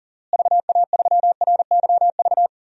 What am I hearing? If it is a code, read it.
Morse code, heard as a keyed tone, about 40 wpm, 700 Hz. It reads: VA3RYV
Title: DXCC-100 Call Signs - 6 Characters - Repeated 2x - 40wpm